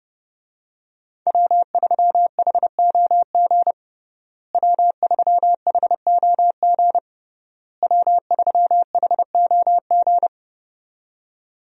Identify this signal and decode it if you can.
Morse code, heard as a keyed tone, about 30 wpm, 700 Hz. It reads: W3HOG W3HOG W3HOG
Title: USA Call Signs - 5 Characters - Repeated 3x - 30wpm